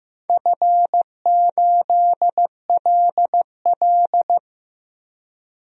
Morse code, heard as a keyed tone, about 15 wpm, 700 Hz. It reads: F8LL
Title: DXCC-100 Call Signs - 4 Characters - Repeated 3x - 15wpm